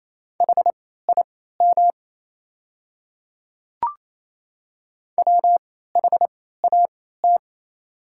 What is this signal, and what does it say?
Morse code, heard as a keyed tone, about 27 wpm, spaced as for 16 wpm, 690 Hz. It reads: HIM WHAT
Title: 3x Character Spacing - Top 100 Words - 28wpm